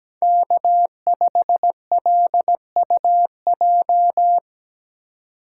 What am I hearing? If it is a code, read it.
Morse code, heard as a keyed tone, about 19 wpm, 700 Hz. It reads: K5LUJ
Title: Speed Racing - USA Call Signs 17wpm